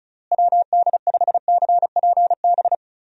Morse code, heard as a keyed tone, about 36 wpm, 700 Hz. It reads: WD5CPB
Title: USA Call Signs - 6 Characters - Repeated 2x - 35wpm